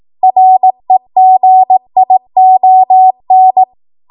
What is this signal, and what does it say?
Morse code, heard as a keyed tone, about 18 wpm, 750 Hz. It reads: REGION
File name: Morse / 18WPM